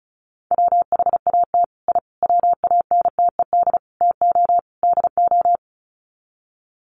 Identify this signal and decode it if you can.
Morse code, heard as a keyed tone, about 34 wpm, 700 Hz. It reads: WHAT I WANTED TO DO
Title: Sets of 5 Words Encore2 35wpm